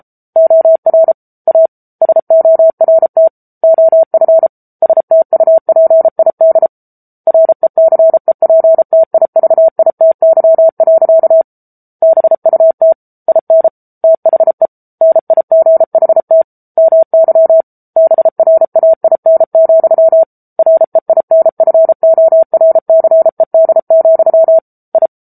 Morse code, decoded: OR A SORT OF STUPID RECEPTIVITY. BUT IN THE NIGHT MY BRAIN, REINFORCED, I

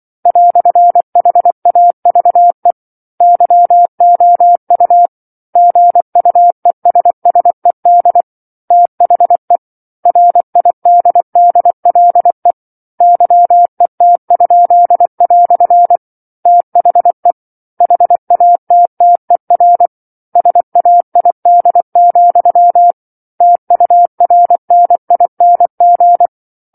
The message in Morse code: "HAVE YOU GUESSED THE RIDDLE YET?" THE HATTER SAID, TURNING

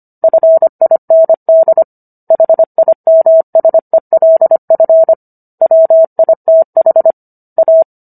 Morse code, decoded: FIND HIMSELF WITH A

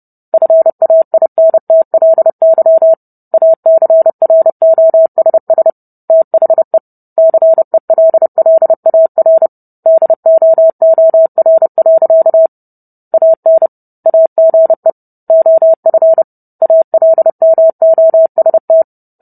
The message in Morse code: FAINTLY ACROSS THE CELLAR DOOR. AN AGE OF ALMOST

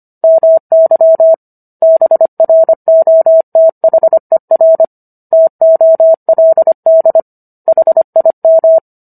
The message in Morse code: MY BROTHER TOLD HIM

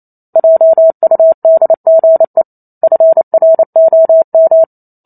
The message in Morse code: JUDGE FROM

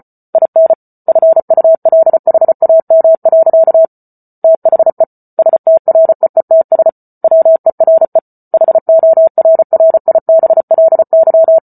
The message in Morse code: IN FULHAM. THE STREETS WERE HORRIBLY

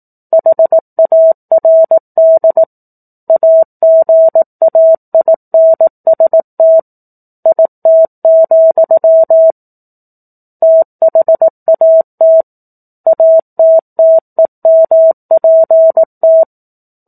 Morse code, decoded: HARD AGAINST IT, THAT ATTEMPT

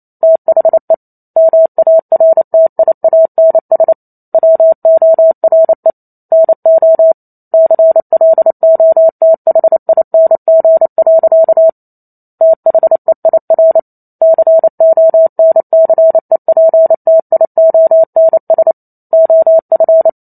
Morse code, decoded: THE MARTIANS WORE NO CLOTHING. THEIR CONCEPTIONS OF

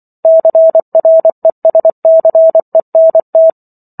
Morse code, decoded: CRESCENT

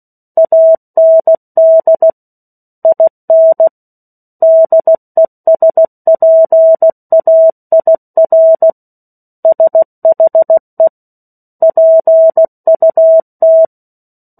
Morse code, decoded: AND IN DESPAIR SHE PUT

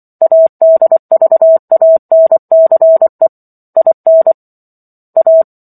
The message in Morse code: ADVANCE IN A

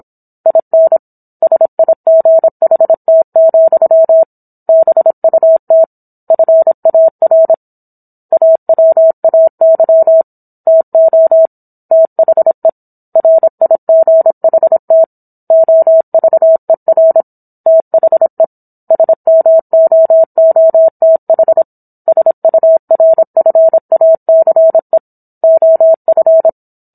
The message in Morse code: IN SIGHT, BUT FAR AWAY TO THE RIGHT OVER THE SMOOTH SURFACE OF

